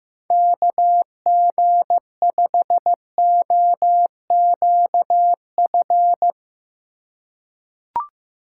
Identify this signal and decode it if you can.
Morse code, heard as a keyed tone, about 16 wpm, 700 Hz. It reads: KG5OQF
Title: USA Call Signs - 6 Characters - 3x Character Spacing - 15wpm